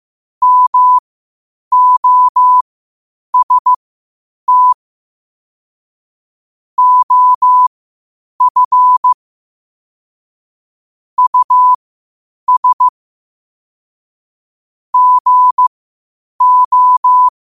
Morse code, heard as a keyed tone, about 15 wpm, spaced as for 8 wpm, 1000 Hz. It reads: MOST OF US GO
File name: qotdfiles / 8wpm